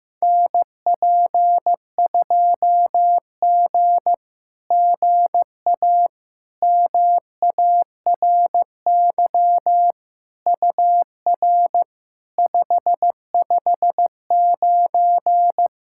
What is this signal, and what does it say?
Morse code, heard as a keyed tone, about 15 wpm, 700 Hz. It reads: NP2G GA MARY UR 559
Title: SOTA QSO Practice 15wpm